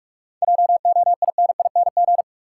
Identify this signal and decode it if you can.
Morse code, heard as a keyed tone, about 48 wpm, 700 Hz. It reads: JOINING
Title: Sets of 5 Words - New Word Drill - 45wpm